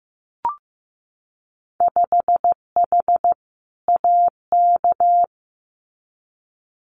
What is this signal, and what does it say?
Morse code, heard as a keyed tone, about 15 wpm, 700 Hz. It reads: 5H AK
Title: Winter Field Day Exchanges Encore 15wpm